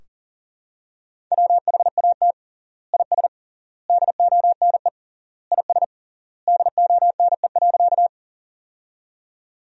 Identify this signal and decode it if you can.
Morse code, heard as a keyed tone, about 40 wpm, 700 Hz. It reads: WHAT IS DONE IS DONE.
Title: 3x Word-Spacing - Sentences from Top 500 Words - 40wpm